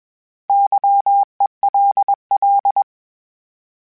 Morse code, heard as a keyed tone, about 20 wpm, 800 Hz. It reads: YELL
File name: Misc / Misc-mp3s / Misc @20